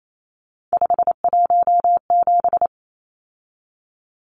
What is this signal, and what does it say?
Morse code, heard as a keyed tone, about 28 wpm, 700 Hz. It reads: 517
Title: Numbers - 3-Digit 28wpm